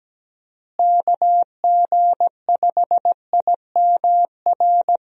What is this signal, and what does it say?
Morse code, heard as a keyed tone, about 17 wpm, 700 Hz. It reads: KG5IMR
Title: USA Call Signs - 6 Characters - Repeated 2x - 17wpm